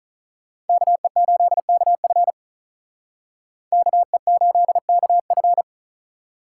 Morse code, heard as a keyed tone, about 43 wpm, 700 Hz. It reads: KE8KF KE8KF
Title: Speed Racing - USA 2x2 Call Signs 35wpm